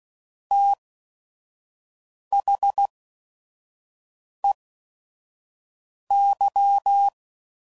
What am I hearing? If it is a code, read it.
Morse code, heard as a keyed tone, about 16 wpm, spaced as for 5 wpm, 800 Hz. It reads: THEY
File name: CW Code Groups @5WPM / Technical Subjects / SwPwrSup